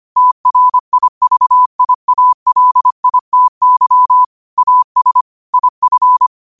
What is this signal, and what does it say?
Morse code, heard as a keyed tone, about 25 wpm, 1000 Hz. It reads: TRIVIALITY AS IF